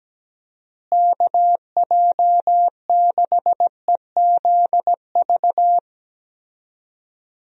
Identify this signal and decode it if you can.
Morse code, heard as a keyed tone, about 17 wpm, 700 Hz. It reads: KJ6EZV
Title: USA Call Signs - 6 Characters - Repeated 2x - 17wpm